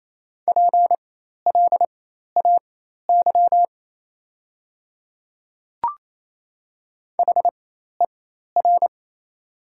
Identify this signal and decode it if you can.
Morse code, heard as a keyed tone, about 28 wpm, spaced as for 13 wpm, 690 Hz. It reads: PLAY HER
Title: 4x Character Spacing - Top 200 Words - 28wpm